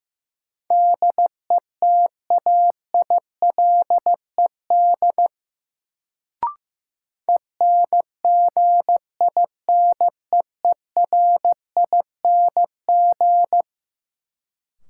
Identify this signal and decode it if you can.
Morse code, heard as a keyed tone, about 15 wpm, 700 Hz. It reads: DETAILED ENGINEERING
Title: Sentences from Top 1500 Words - New Word Drill Alt - 15wpm